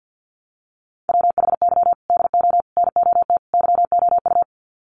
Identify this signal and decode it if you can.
Morse code, heard as a keyed tone, about 50 wpm, 700 Hz. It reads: WHY DO NOT YOU
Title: Sets of 4 Words 50wpm